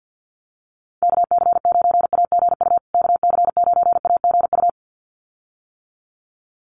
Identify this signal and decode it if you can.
Morse code, heard as a keyed tone, about 50 wpm, 700 Hz. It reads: KC9AGU KC9AGU
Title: USA Call Signs Repeated 2x 50wpm